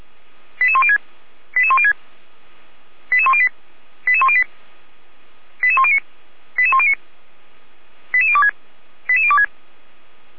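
Five-tone selective call (ZVEI), heard as quick runs of five tones, tones of about 70 ms. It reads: 88106 88106 88108 88108 88109 88109 88026 88026
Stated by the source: ZVEI five-tone call 88103 received with an SDR, archived on the Feuerwehr Orlamünde audio server.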